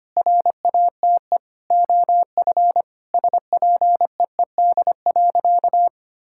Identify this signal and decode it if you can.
Morse code, heard as a keyed tone, about 25 wpm, 700 Hz. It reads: RATE OF SPEED.
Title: Sentences from Top 1000 Words - Review Part 5 - 25wpm